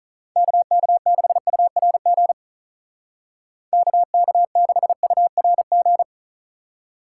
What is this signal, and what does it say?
Morse code, heard as a keyed tone, about 45 wpm, 700 Hz. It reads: KK6URG KK6URG